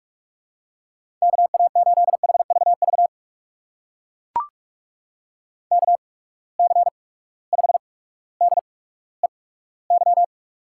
Morse code, heard as a keyed tone, about 44 wpm, spaced as for 19 wpm, 700 Hz. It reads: KA8HVV KC5DEY
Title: USA Call Signs - 8x Character Spacing - 45wpm